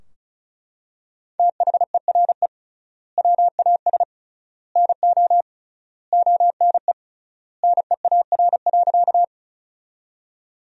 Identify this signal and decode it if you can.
Morse code, heard as a keyed tone, about 35 wpm, 700 Hz. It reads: THERE WAS NO ONE NEAR.
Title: 3x Word-Spacing - Sentences from Top 200 Words - 35wpm